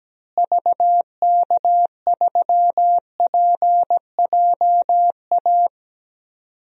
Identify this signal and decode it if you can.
Morse code, heard as a keyed tone, about 17 wpm, 700 Hz. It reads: VK3PJA
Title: DXCC-100 Call Signs - 6 Characters - Repeated 2x - 17wpm